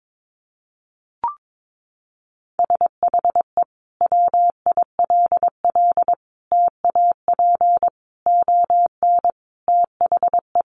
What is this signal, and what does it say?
Morse code, heard as a keyed tone, about 22 wpm, 700 Hz. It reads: SHE WILL TAP ON THE
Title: Sentences from Top 1400 Words 22wpm